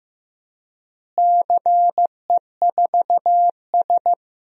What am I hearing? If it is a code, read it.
Morse code, heard as a keyed tone, about 15 wpm, 700 Hz. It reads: CE4S
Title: DXCC-100 Call Signs - 4 Characters - Repeated 2x - 15wpm